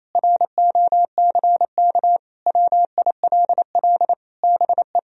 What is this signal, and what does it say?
Morse code, heard as a keyed tone, about 28 wpm, 700 Hz. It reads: ROCK WILL BE
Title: Sentences from Top 1000 Words - Review Part 2 - 28wpm